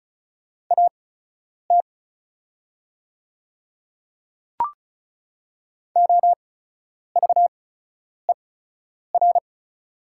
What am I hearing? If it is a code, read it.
Morse code, heard as a keyed tone, about 36 wpm, spaced as for 10 wpm, 690 Hz. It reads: AT OVER
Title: 8x Character Spacing - Top 100 Words - 35wpm